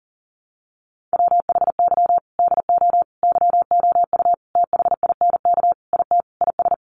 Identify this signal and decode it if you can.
Morse code, heard as a keyed tone, about 40 wpm, 700 Hz. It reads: WHY DO YOU THINK IT IS
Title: Sentences from Top 1400 Words 40wpm